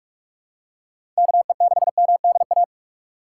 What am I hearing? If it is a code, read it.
Morse code, heard as a keyed tone, about 47 wpm, 700 Hz. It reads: KE6MDA
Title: USA Call Signs - 4x Character Spacing - 45wpm